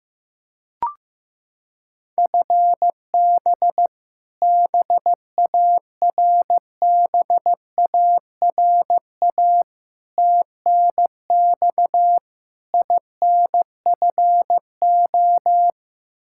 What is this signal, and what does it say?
Morse code, heard as a keyed tone, about 16 wpm, 700 Hz. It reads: FB BARBARA TNX INFO
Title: Practice QSO - Exchange 3 - More Info - 15wpm